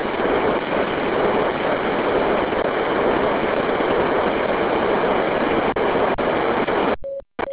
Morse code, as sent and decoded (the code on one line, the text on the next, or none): none